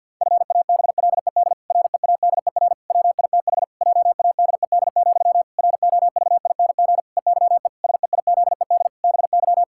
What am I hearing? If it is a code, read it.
Morse code, heard as a keyed tone, about 50 wpm, 700 Hz. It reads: FABLED READER WITH JADED, ROVING EYE SEIZED BY
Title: Pangram Challenge v2 - 50wpm